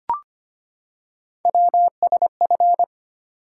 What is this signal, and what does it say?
Morse code, heard as a keyed tone, about 25 wpm, 700 Hz. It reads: WSF
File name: Three Letter-Number Combinations - 25wpm